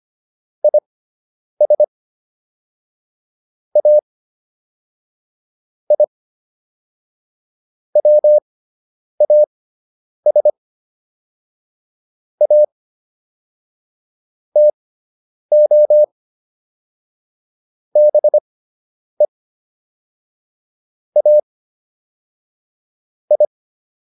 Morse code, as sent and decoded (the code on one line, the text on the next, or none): .. ... / .- / .. / .-- .- ... / .- / - --- / -... / .- / ..
IS A I WAS A TO BE A I